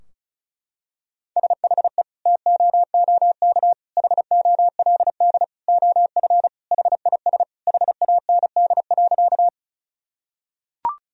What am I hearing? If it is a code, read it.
Morse code, heard as a keyed tone, about 34 wpm, 700 Hz. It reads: SHE TOOK HOLD OF HIS HAND.
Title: Sentences from Top 1000 Words - Review Part 2 - 35wpm